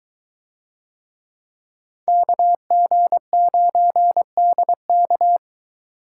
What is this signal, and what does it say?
Morse code, heard as a keyed tone, about 22 wpm, 700 Hz. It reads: KG9DK